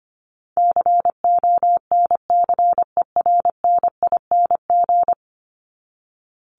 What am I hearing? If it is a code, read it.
Morse code, heard as a keyed tone, about 25 wpm, 700 Hz. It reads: CONCERNING
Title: ING Suffix - Top 100 Words with Base Words 25wpm